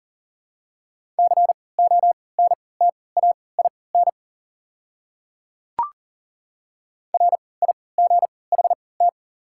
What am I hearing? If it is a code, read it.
Morse code, heard as a keyed tone, about 40 wpm, spaced as for 23 wpm, 690 Hz. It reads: CONTAIN RIGHT
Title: 3x Character Spacing - Top 500 Words - 40wpm